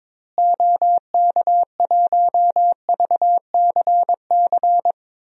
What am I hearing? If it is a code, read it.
Morse code, heard as a keyed tone, about 22 wpm, 700 Hz. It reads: OK1VCC
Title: DXCC-100 Call Signs - 6 Characters - Repeated 2x - 22wpm